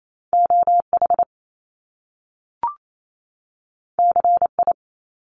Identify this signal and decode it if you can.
Morse code, heard as a keyed tone, about 28 wpm, 700 Hz. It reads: OH CI
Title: Top Two Letter Combinations in Words 28wpm